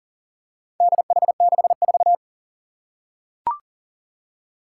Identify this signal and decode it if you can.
Morse code, heard as a keyed tone, about 38 wpm, 690 Hz. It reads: DH64
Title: Four Challenging Letter-Number Combinations 40wpm